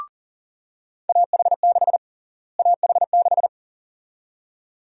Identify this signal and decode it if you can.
Morse code, heard as a keyed tone, about 37 wpm, 700 Hz. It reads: AH6 AH6
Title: USA Call Sign Prefixes - Repeated 2x - 40wpm